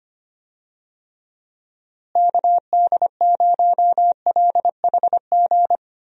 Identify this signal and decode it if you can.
Morse code, heard as a keyed tone, about 25 wpm, 700 Hz. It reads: KD0LHG
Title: USA Call Signs - 6 Characters - Repeated 2x - 25wpm